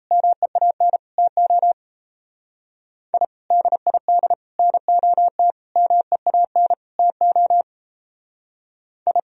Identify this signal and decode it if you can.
Morse code, heard as a keyed tone, about 38 wpm, 700 Hz. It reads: MEAN TO I DID NOT MEAN TO I